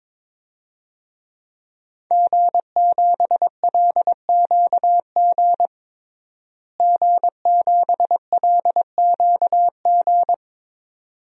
Morse code, heard as a keyed tone, about 22 wpm, 700 Hz. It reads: G7LQG G7LQG
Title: DXCC-100 Call Signs - 5 Characters - Repeated 2x - 22wpm